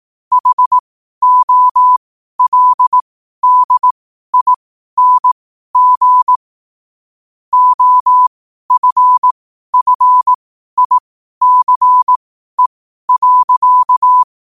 Morse code, heard as a keyed tone, about 18 wpm, spaced as for 12 wpm, 1000 Hz. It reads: HOLDING OFFICE.
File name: qotdfiles / 12wpm